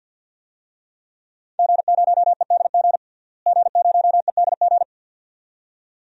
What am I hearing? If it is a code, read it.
Morse code, heard as a keyed tone, about 51 wpm, 700 Hz. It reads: G0EDG G0EDG
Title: DXCC-100 Call Signs - 5 Characters - Repeated 2x - 50wpm